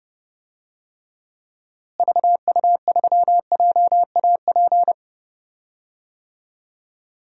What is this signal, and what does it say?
Morse code, heard as a keyed tone, about 30 wpm, 700 Hz. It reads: VU3JAP
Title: DXCC-100 Call Signs - 6 Characters - Repeated 3x - 30wpm